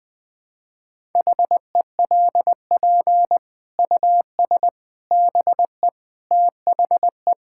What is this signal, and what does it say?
Morse code, heard as a keyed tone, about 20 wpm, 700 Hz. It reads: HELP US BE THE
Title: Sets of 9 Words - 20wpm